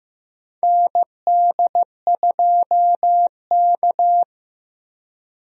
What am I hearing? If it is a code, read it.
Morse code, heard as a keyed tone, about 15 wpm, 700 Hz. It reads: ND2K
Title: USA Call Signs - 4 Characters - 15wpm